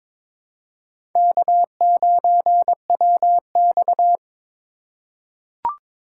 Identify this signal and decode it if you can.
Morse code, heard as a keyed tone, about 22 wpm, 700 Hz. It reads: K9WX
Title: USA Call Signs - 2x Character Spacing - 22wpm